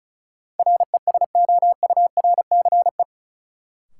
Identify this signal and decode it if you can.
Morse code, heard as a keyed tone, about 34 wpm, 700 Hz. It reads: RESOURCE
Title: Sentences from Top 1300 Words - New Word Drill Alt - 35wpm